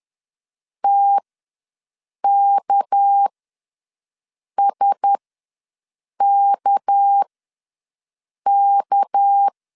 Morse code, heard as a keyed tone, about 10 wpm, spaced as for 5 wpm, 800 Hz. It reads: TKSKK
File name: LEZIONI / 6 PKX0 / 6 PK ws 10wpm 3745